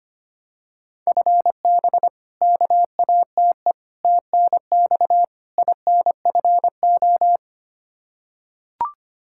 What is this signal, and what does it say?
Morse code, heard as a keyed tone, about 25 wpm, 700 Hz. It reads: FB KATE TNX INFO
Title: Practice QSO - Exchange 3 - More Info - 25wpm